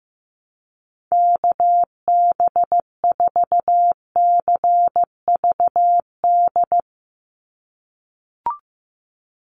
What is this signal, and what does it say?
Morse code, heard as a keyed tone, about 15 wpm, 700 Hz. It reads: KB4CVD
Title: USA Call Signs 15wpm